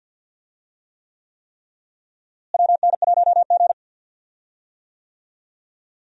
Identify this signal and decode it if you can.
Morse code, heard as a keyed tone, about 51 wpm, 700 Hz. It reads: WN1G
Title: USA Call Signs - 4 Characters - Repeated 3x - 50wpm